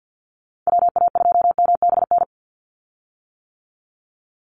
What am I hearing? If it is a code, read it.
Morse code, heard as a keyed tone, about 50 wpm, 700 Hz. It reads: WA2MBN
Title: USA Call Signs 50wpm